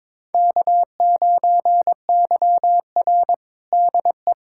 Morse code, heard as a keyed tone, about 22 wpm, 700 Hz. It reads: K9YR DE
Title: USA Callsign DE Callsign - 4 Characters - 22wpm